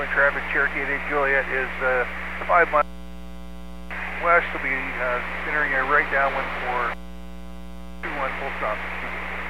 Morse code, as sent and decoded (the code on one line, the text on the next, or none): none